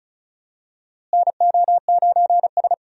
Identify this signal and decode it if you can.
Morse code, heard as a keyed tone, about 36 wpm, 700 Hz. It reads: NO9S
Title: USA Call Signs - 4 Characters - Repeated 3x - 35wpm